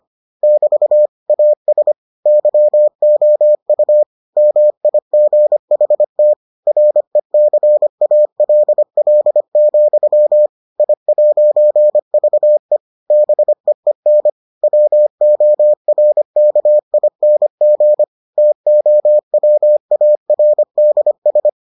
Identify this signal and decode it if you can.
Morse code, heard as a keyed tone, about 25 wpm, 600 Hz. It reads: <BT> AS YOU MIGHT RECALL, I'VE BEEN WORKING TOWARDS